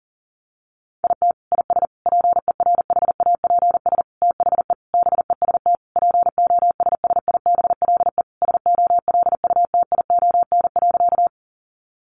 Morse code, heard as a keyed tone, about 40 wpm, 700 Hz. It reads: IT IS PERHAPS THE BEST POSSIBLE SOLUTION.
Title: Sentences from Top 1300 Words 40wpm